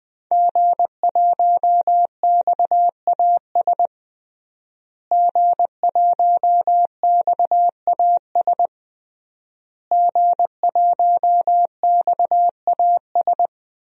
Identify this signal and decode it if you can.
Morse code, heard as a keyed tone, about 20 wpm, 700 Hz. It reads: G1XAS G1XAS G1XAS
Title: DXCC-100 Call Signs - 5 Characters - Repeated 3x - 20wpm